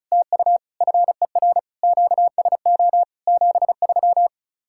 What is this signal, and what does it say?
Morse code, heard as a keyed tone, about 45 wpm, 700 Hz. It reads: TU FER QSO 73